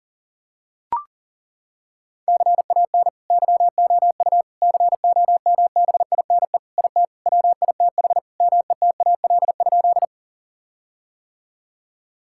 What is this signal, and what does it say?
Morse code, heard as a keyed tone, about 40 wpm, 700 Hz. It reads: CAN YOU COMBINE IT WITH METAL?
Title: Sentences from Top 1600 Words 40wpm